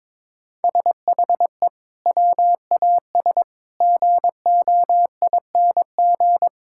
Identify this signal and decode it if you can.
Morse code, heard as a keyed tone, about 22 wpm, 700 Hz. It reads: SHE WAS GOING
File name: practice / 22wpm / Sentences with 4-Word Phrases - 22wpm